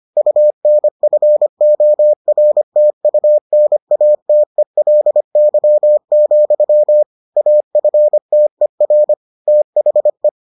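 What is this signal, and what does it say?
Morse code, heard as a keyed tone, about 25 wpm, 600 Hz. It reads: UNFORTUNATELY, AFTER THE
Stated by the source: Morse edition of the Foundations of Amateur Radio episodes.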